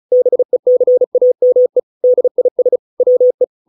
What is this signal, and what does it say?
Morse code, heard as a keyed tone, about 36 wpm, 500 Hz. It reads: BECAME DIS WE